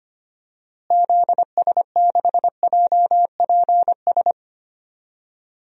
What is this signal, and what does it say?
Morse code, heard as a keyed tone, about 25 wpm, 700 Hz. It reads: ZS6JPS
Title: DXCC-100 Call Signs - 6 Characters - Repeated 3x - 25wpm